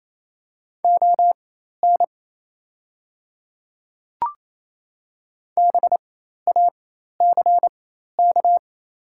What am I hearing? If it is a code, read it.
Morse code, heard as a keyed tone, about 28 wpm, spaced as for 13 wpm, 690 Hz. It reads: ON BACK
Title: 4x Character Spacing - Top 300 Words - 28wpm